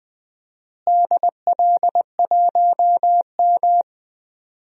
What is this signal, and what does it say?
Morse code, heard as a keyed tone, about 20 wpm, 700 Hz. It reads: DL1M
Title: DXCC-100 Call Signs - 4 Characters - Repeated 2x - 20wpm